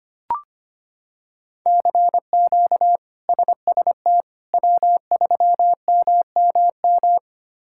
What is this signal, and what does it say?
Morse code, heard as a keyed tone, about 25 wpm, 700 Hz. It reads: CQ SST W3MMM
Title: K1USN SST QSO Practice - 25wpm